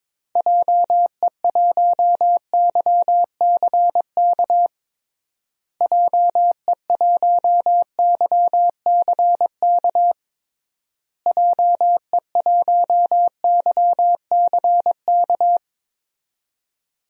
Morse code, heard as a keyed tone, about 22 wpm, 700 Hz. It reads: JE1YCK JE1YCK JE1YCK
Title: DXCC-100 Call Signs - 6 Characters - Repeated 3x - 22wpm